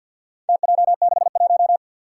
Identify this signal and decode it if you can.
Morse code, heard as a keyed tone, about 51 wpm, 700 Hz. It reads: TJ61
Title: Four Challenging Letter-Number Combinations 50wpm